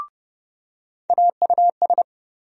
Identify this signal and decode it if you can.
Morse code, heard as a keyed tone, about 30 wpm, 700 Hz. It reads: AUS
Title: USA Call Sign Postfixes - 3 Letters 30wpm